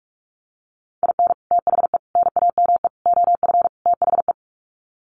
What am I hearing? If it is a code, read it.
Morse code, heard as a keyed tone, about 44 wpm, 700 Hz. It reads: IN THE NAME OF THE
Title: Sets of 5 Words Encore 45wpm